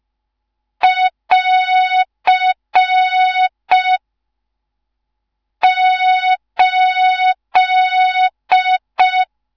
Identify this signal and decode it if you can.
Morse code, heard as a keyed tone, about 5 wpm, 3700 Hz. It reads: <AR> 8
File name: Morze-HA2MN / hang / 14 280 HA2MN